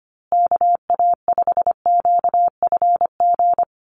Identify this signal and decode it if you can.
Morse code, heard as a keyed tone, about 25 wpm, 700 Hz. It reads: KA5QFG
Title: USA Call Signs Repeated 3x 25wpm